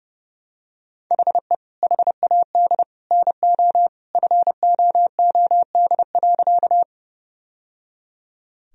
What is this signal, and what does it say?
Morse code, heard as a keyed tone, about 30 wpm, 700 Hz. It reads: HE HAD NO FOOD.
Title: Sentences from Top 1000 Words - Review Part 1 - 30wpm